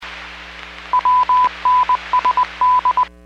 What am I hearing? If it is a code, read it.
Morse code, heard as a keyed tone, about 20 wpm, 1000 Hz. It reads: WNSD
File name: 13-40 WPS 451.25 WPS